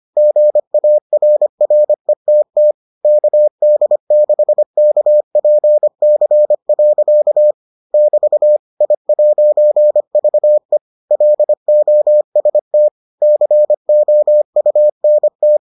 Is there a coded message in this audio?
Yes: Morse code from 0.2 to 15.6 s, heard as a keyed tone, about 25 wpm, 600 Hz. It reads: GARRETT KD6KPC. <BT> I'VE LOST COUNT